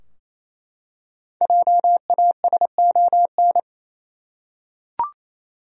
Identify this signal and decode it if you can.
Morse code, heard as a keyed tone, about 27 wpm, 700 Hz. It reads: JASON
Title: USA Common Names - Max 5 Letters - 28wpm